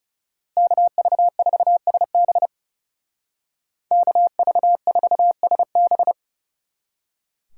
Morse code, heard as a keyed tone, about 35 wpm, 700 Hz. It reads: KV4SB KV4SB
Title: Speed Racing - USA 2x2 Call Signs 30wpm